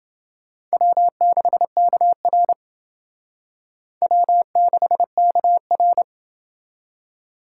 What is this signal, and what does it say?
Morse code, heard as a keyed tone, about 29 wpm, 700 Hz. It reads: W6KR W6KR